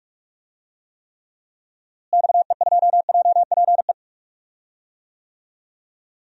Morse code, heard as a keyed tone, about 43 wpm, 700 Hz. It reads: XE2JPE
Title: DXCC-100 Call Signs - 6 Characters - Repeated 2x - 45wpm